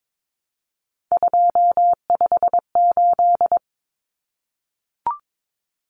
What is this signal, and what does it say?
Morse code, heard as a keyed tone, about 22 wpm, 700 Hz. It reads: 258
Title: Numbers - 3-Digit 22wpm